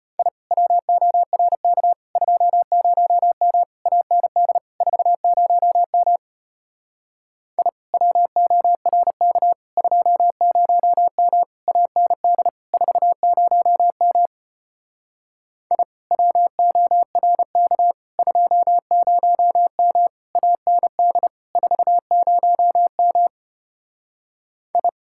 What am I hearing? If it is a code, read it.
Morse code, heard as a keyed tone, about 33 wpm, 700 Hz. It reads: I WORK 20M AND 40M I WORK 20M AND 40M I WORK 20M AND 40M I